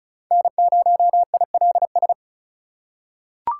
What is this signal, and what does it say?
Morse code, heard as a keyed tone, about 36 wpm, 700 Hz. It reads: N0ILS
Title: USA Call Signs - 5 Characters - 2x Character Spacing - 35wpm